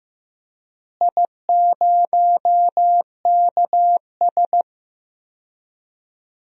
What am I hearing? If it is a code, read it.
Morse code, heard as a keyed tone, about 15 wpm, 700 Hz. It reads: I0KS
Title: DXCC-100 Call Signs - 4 Characters - Repeated 2x - 15wpm